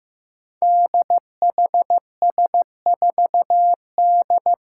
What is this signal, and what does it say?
Morse code, heard as a keyed tone, about 15 wpm, 700 Hz. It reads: DHS4D
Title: Five Challenging Letter-Number Combinations 15wpm